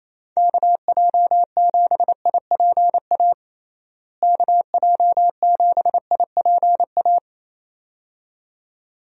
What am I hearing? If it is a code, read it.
Morse code, heard as a keyed tone, about 28 wpm, 700 Hz. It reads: KJ7IPA KJ7IPA
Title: USA Call Signs - 6 Characters - Repeated 2x - 28wpm